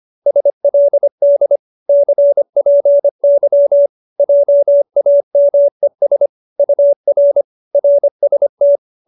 Morse code, heard as a keyed tone, about 25 wpm, 570 Hz. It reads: SLD CPY JAMES UR RST
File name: qso207b 25